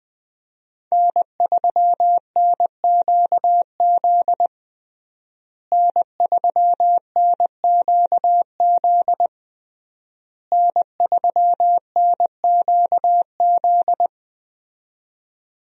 Morse code, heard as a keyed tone, about 21 wpm, 700 Hz. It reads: N3NQZ N3NQZ N3NQZ
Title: USA Call Signs - 5 Characters - Repeated 3x - 20wpm